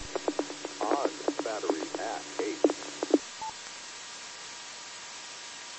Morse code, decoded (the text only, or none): none